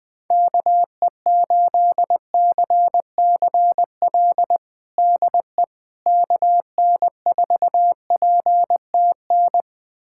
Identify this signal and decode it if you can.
Morse code, heard as a keyed tone, about 20 wpm, 700 Hz. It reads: KE8CCL DE KN4PTN
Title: USA Callsign DE Callsign - 6 Characters - 20wpm